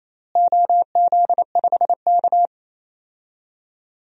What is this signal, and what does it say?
Morse code, heard as a keyed tone, about 28 wpm, 700 Hz. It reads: OZ5K
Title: DXCC-100 Call Signs - 4 Characters - Repeated 3x - 28wpm